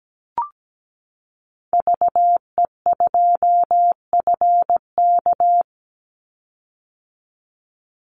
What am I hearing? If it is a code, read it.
Morse code, heard as a keyed tone, about 17 wpm, 700 Hz. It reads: VE2FK
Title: Call Signs - 17wpm